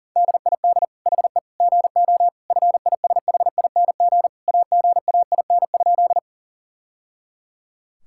Morse code, heard as a keyed tone, about 40 wpm, 700 Hz. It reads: DID HE GO FISHING AGAIN?
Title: Sentences from Top 1000 Words - Review Part 5 - 40wpm